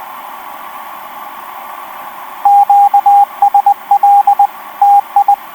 Morse code, heard as a keyed tone, about 20 wpm, 810 Hz, with radio noise. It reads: QSL TI
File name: v5-dk1ce f4ffh etc 10m cw 05jan2011 1458